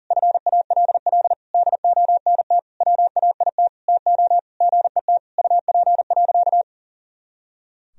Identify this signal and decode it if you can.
Morse code, heard as a keyed tone, about 37 wpm, 700 Hz. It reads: FALL DONT WAIT TO GET UP.
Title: Sentences from Top 500 Words and Abbreviations 40wpm